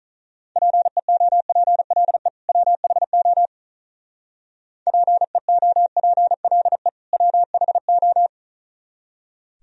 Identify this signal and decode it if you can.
Morse code, heard as a keyed tone, about 42 wpm, 700 Hz. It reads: PEOPLE WHO PEOPLE WHO
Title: Speed-Racing - Sets of 2 Words 35wpm